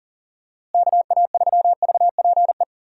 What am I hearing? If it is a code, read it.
Morse code, heard as a keyed tone, about 40 wpm, 700 Hz. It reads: KA3VPE
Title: USA Call Signs - 6 Characters - Repeated 3x - 40wpm